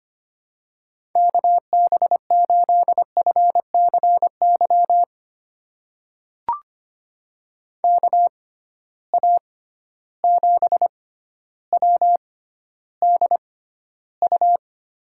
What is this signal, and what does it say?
Morse code, heard as a keyed tone, about 25 wpm, spaced as for 13 wpm, 700 Hz. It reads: KB8FCY KA7WDU
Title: USA Call Signs - 6 Characters - 6x Character Spacing - 25wpm